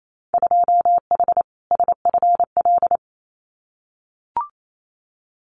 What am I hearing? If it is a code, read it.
Morse code, heard as a keyed tone, about 28 wpm, 700 Hz. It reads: 2H SFL
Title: Winter Field Day Exchanges Encore 28wpm